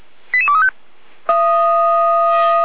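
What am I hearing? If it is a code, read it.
Five-tone selective call (ZVEI), heard as quick runs of five tones, tones of about 70 ms. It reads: 88235
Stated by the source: ZVEI five-tone call 88288 received by SDR on channel 455.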